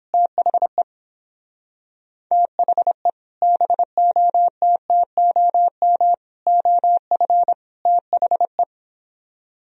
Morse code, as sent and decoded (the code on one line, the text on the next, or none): - .... / - .... / -... --- - - --- -- / --- ..-. / - .... .
THE THE BOTTOM OF THE